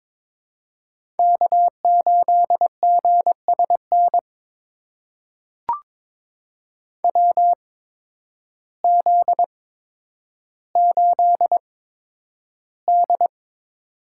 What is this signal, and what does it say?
Morse code, heard as a keyed tone, about 22 wpm, spaced as for 10 wpm, 700 Hz. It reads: K8GSN WZ8D
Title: USA Call Signs - 8x Character Spacing - 22wpm